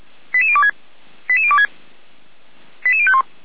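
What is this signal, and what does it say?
Five-tone selective call (ZVEI), heard as quick runs of five tones, tones of about 70 ms. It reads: 88026 88026 88051
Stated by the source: ZVEI five-tone selective call 88014, received by SDR.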